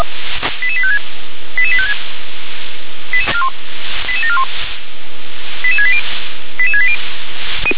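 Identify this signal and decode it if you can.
Five-tone selective call (ZVEI), heard as quick runs of five tones, tones of about 70 ms. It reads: 88856 88856 88741 88680 88680
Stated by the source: Five-tone ZVEI selective call, received on an SDR.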